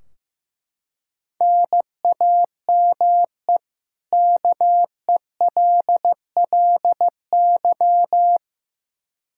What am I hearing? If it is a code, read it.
Morse code, heard as a keyed tone, about 15 wpm, 700 Hz. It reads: NAME KELLY
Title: Practice QSO - Exchange 1 - Beginning - 15wpm